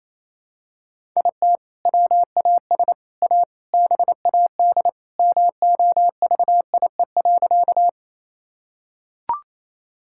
Morse code, heard as a keyed tone, about 28 wpm, 700 Hz. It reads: IT WAS A BAD MOVIE.
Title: Sentences from Top 1000 Words - Review Part 5 - 28wpm